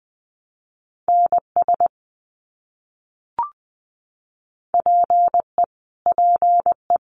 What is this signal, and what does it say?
Morse code, heard as a keyed tone, about 20 wpm, 700 Hz. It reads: NS PE PE
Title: Canadian Provinces and Territories - Repeated 3x - 20wpm